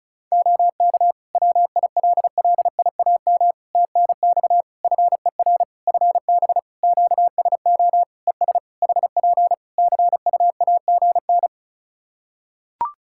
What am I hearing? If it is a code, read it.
Morse code, heard as a keyed tone, about 36 wpm, 700 Hz. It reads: OK WILLIAM TNX FER FB QSO ES HP CUAGN
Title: Practice QSO - Exchange 4 - Ending - 35wpm